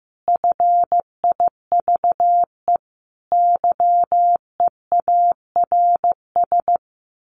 Morse code, heard as a keyed tone, about 15 wpm, 700 Hz. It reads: FIVE YEARS
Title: Sets of 5 Words Encore 15wpm